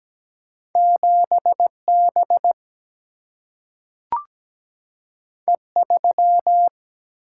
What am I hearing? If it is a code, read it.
Morse code, heard as a keyed tone, about 17 wpm, 700 Hz. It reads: 7 B E 3
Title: Single-Letter Single-Number 17wpm